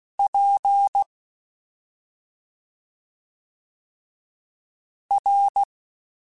Morse code, heard as a keyed tone, about 16 wpm, spaced as for 2 wpm, 800 Hz. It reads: PR
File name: Special INGs-mp3s / Special INGs-mp3s / Slow ing @10